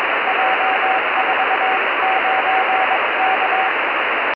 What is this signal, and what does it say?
Morse code, heard as a keyed tone, about 21 wpm, 760 Hz, with radio noise. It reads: J49K
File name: 70mhz / j49k pe1itr 70200khz